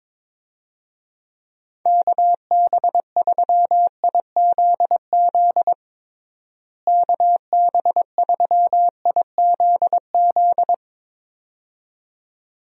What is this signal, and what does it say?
Morse code, heard as a keyed tone, about 22 wpm, 700 Hz. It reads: KB3IZZ KB3IZZ
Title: USA Call Signs - 6 Characters - Repeated 2x - 22wpm